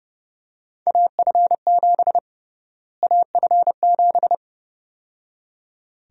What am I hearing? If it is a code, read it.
Morse code, heard as a keyed tone, about 32 wpm, 700 Hz. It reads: AF7 AF7
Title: USA Call Sign Prefixes - Repeated 2x - 30wpm